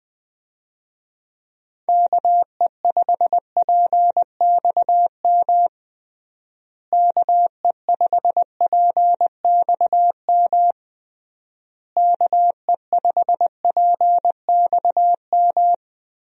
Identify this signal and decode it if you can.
Morse code, heard as a keyed tone, about 20 wpm, 700 Hz. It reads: KE5PXM KE5PXM KE5PXM
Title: USA Call Signs - 6 Characters - Repeated 3x - 20wpm